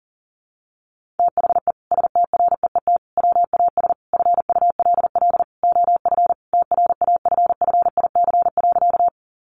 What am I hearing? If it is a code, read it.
Morse code, heard as a keyed tone, about 40 wpm, 700 Hz. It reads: THE STREET WAS FULL OF TRAFFIC.
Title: Sentences from Top 1300 Words 40wpm